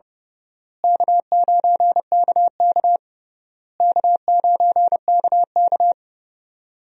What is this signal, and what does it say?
Morse code, heard as a keyed tone, about 32 wpm, 700 Hz. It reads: K9KK K9KK
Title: USA Call Signs - 4 Characters - Repeated 3x - 30wpm